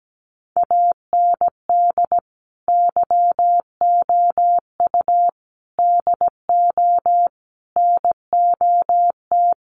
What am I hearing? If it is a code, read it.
Morse code, heard as a keyed tone, about 17 wpm, 700 Hz. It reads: AND YOU DO NOT
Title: Sets of 4 Words 17wpm